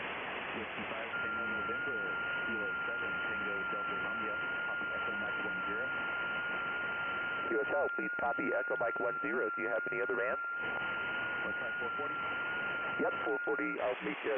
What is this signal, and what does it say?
FT8, with speech.